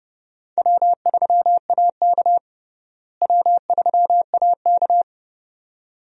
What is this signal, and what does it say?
Morse code, heard as a keyed tone, about 30 wpm, 700 Hz. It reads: W3AK W3AK
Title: USA Call Signs - 4 Characters - Repeated 2x - 30wpm